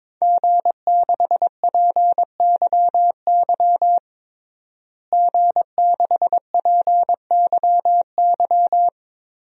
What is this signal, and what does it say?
Morse code, heard as a keyed tone, about 22 wpm, 700 Hz. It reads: G6PYY G6PYY
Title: DXCC-100 Call Signs - 5 Characters - Repeated 3x - 22wpm